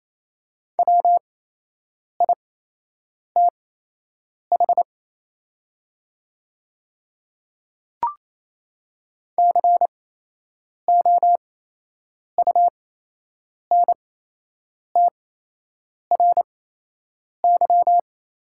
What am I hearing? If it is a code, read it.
Morse code, heard as a keyed tone, about 28 wpm, spaced as for 8 wpm, 700 Hz. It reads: WITH COUNTRY
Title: 8x Character Spacing - Top 500 Words - 28wpm